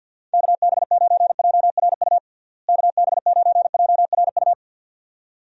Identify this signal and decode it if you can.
Morse code, heard as a keyed tone, about 45 wpm, 700 Hz. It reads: KB9JRU KB9JRU
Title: USA Call Signs - 6 Characters - Repeated 2x - 50wpm